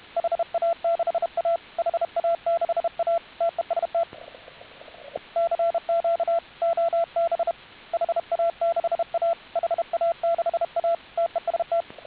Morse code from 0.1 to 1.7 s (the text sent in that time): HA6A